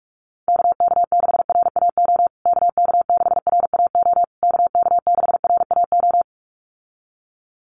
Morse code, heard as a keyed tone, about 46 wpm, 690 Hz. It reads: KK6RAO KK6RAO KK6RAO
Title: USA Call Signs Repeated 3x 45wpm